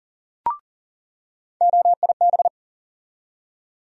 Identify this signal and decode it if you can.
Morse code, heard as a keyed tone, about 40 wpm, 700 Hz. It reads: OIB
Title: USA Call Sign Postfixes - 3 Letters 40wpm